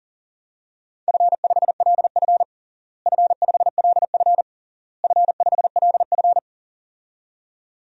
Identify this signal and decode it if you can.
Morse code, heard as a keyed tone, about 40 wpm, 700 Hz. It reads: F5LF F5LF F5LF
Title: DXCC-100 Call Signs - 4 Characters - Repeated 3x - 40wpm